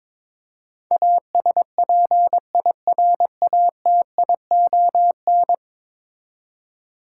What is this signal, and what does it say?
Morse code, heard as a keyed tone, about 22 wpm, 700 Hz. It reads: ASPIRATION
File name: practice / 22wpm / 10 Letter Words - First 3 Letters Given - 22wpm